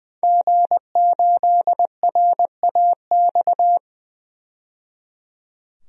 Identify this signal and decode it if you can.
Morse code, heard as a keyed tone, about 20 wpm, 700 Hz. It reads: G8RAX
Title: DXCC-100 Call Signs - 5 Characters - 20wpm